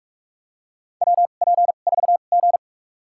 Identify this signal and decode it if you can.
Morse code, heard as a keyed tone, about 45 wpm, spaced as for 32 wpm, 700 Hz. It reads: WP4G
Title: USA Call Signs - 4 Characters - 2x Character Spacing - 45wpm